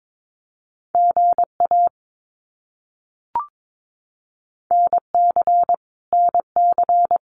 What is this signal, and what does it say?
Morse code, heard as a keyed tone, about 22 wpm, 700 Hz. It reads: GA NC NC
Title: US States Repeated 3x 22wpm